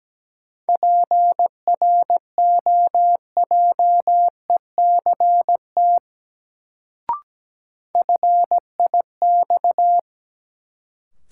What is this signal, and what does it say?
Morse code, heard as a keyed tone, about 17 wpm, 700 Hz. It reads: PROJECT FIX
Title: Sentences from Top 1400 Words - New Word Drill - 17wpm